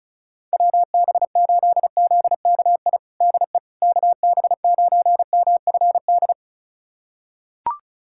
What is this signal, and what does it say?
Morse code, heard as a keyed tone, about 36 wpm, 700 Hz. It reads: WB8ZKI DE KB9MFD
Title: USA Callsign DE Callsign - 6 Characters - 35wpm